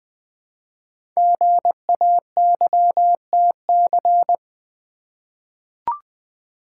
Morse code, heard as a keyed tone, about 20 wpm, 700 Hz. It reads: GAYTC
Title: Code Groups - 5 Letters - 20wpm